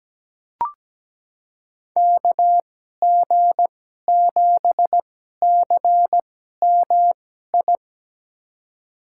Morse code, heard as a keyed tone, about 17 wpm, spaced as for 12 wpm, 700 Hz. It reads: KG7CMI
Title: USA Call Signs - 6 Characters - 2x Character Spacing - 17wpm